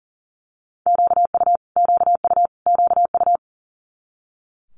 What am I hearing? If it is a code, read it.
Morse code, heard as a keyed tone, about 40 wpm, 700 Hz. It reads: QU QU QU
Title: Multi-letter Phonemes Repeated 3x 40wpm